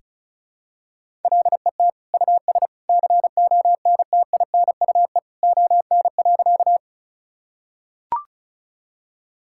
Morse code, heard as a keyed tone, about 34 wpm, 690 Hz. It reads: LET US CONTINUE ON.
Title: Sentences from Top 1000 Words - Review Part 6 - 35wpm